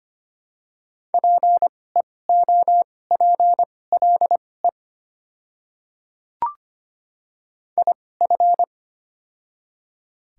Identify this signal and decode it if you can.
Morse code, heard as a keyed tone, about 25 wpm, spaced as for 18 wpm, 690 Hz. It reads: PEOPLE IF
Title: 2x Character Spacing - Top 300 Words - 25wpm